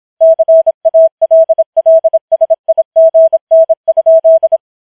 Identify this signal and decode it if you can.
Morse code, heard as a keyed tone, about 26 wpm, 650 Hz. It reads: CALLSIGN?
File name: challenge191